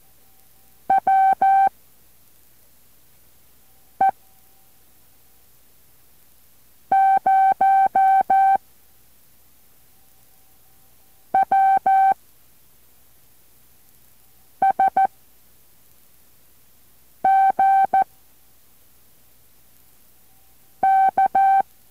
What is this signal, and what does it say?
Morse code, heard as a keyed tone, about 14 wpm, 770 Hz. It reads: W E 0 W S G K
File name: intl morse code2